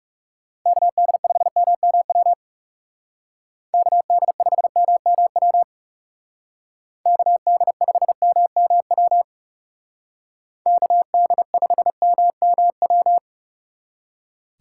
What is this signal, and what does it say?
Morse code, heard as a keyed tone, about 35 wpm, 700 Hz. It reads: KD5MMW KD5MMW KD5MMW KD5MMW
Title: Speed Racing - USA 2x3 Call Signs 30wpm